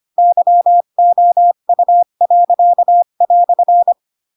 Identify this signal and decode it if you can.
Morse code, heard as a keyed tone, about 25 wpm, 700 Hz. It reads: YOU."